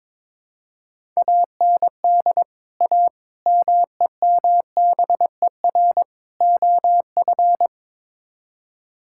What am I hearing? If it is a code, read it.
Morse code, heard as a keyed tone, about 22 wpm, 700 Hz. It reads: AND A MEMBER OF
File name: practice / 22wpm / Build A Phrase - 5-Word Phrases - 22wpm